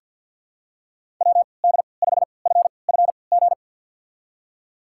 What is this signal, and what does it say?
Morse code, heard as a keyed tone, about 52 wpm, spaced as for 29 wpm, 700 Hz. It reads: WD5FFG
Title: USA Call Signs - 6 Characters - 3x Character Spacing - 50wpm